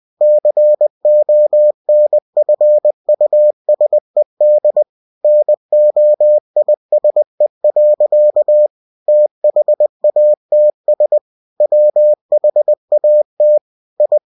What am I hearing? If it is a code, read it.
Morse code, heard as a keyed tone, about 20 wpm, 600 Hz. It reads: CONFUSED NOISE. THATS WHAT I
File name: cw-books / winnie-the-pooh / winnie-20wpm-page-0079